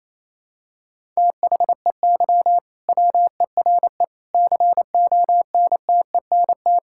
Morse code, heard as a keyed tone, about 28 wpm, 700 Hz. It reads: THEY WERE CONTENT
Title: Sentences from Top 1000 Words - Review Part 6 - 28wpm